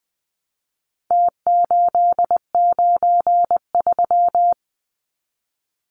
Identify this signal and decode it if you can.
Morse code, heard as a keyed tone, about 20 wpm, 700 Hz. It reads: T893
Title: Numbers - Contest Style - 4-Digit 20wpm